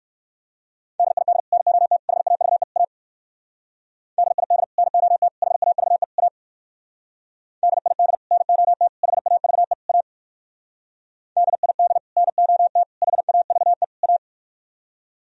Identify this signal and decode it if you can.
Morse code, heard as a keyed tone, about 57 wpm, 700 Hz. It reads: DID NOT HAVE A DID NOT HAVE A DID NOT HAVE A DID NOT HAVE A